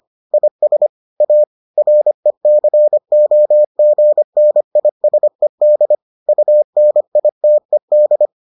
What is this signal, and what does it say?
Morse code, heard as a keyed tone, about 25 wpm, 600 Hz. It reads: IS A RECOGNISED UNITED